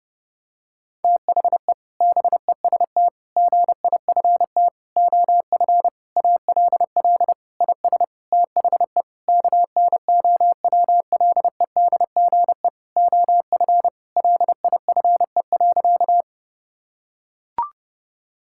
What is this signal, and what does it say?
Morse code, heard as a keyed tone, about 30 wpm, 700 Hz. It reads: THE BEST GIFT OF ALL IS THE KNOWLEDGE OF LIFE.
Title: Sentences from Top 1000 Words - Review Part 6 - 30wpm